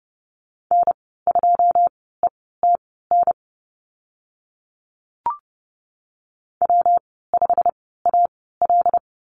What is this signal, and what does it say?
Morse code, heard as a keyed tone, about 30 wpm, spaced as for 17 wpm, 700 Hz. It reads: N2ETN W5AL
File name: Lesson 19g - Letter w - Call Signs 30wpm